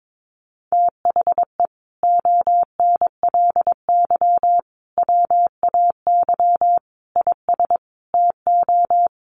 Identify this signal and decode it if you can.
Morse code, heard as a keyed tone, about 22 wpm, 700 Hz. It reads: THE ONLY WAY IS TO
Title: Sentences from Top 1500 Words 22wpm